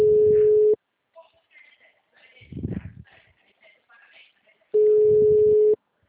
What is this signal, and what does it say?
Ringback tone (Europe, 425 Hz), with speech.
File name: out- -2012- - - .35512